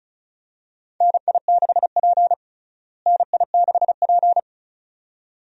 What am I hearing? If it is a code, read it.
Morse code, heard as a keyed tone, about 34 wpm, 700 Hz. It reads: NI6P NI6P
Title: USA Call Signs - 4 Characters - Repeated 2x - 35wpm